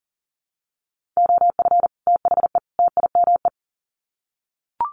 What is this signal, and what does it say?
Morse code, heard as a keyed tone, about 40 wpm, 700 Hz. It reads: OF THE TIME
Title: Sets of 3 Words 40wpm